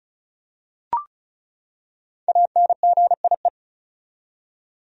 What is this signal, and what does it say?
Morse code, heard as a keyed tone, about 36 wpm, 700 Hz. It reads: ANGIE
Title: USA Common Names - 35wpm